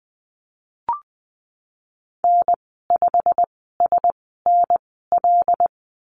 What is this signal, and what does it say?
Morse code, heard as a keyed tone, about 19 wpm, spaced as for 14 wpm, 700 Hz. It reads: N5SNL
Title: 122 Lesson 20g - Number 3 - Call Signs 20wpm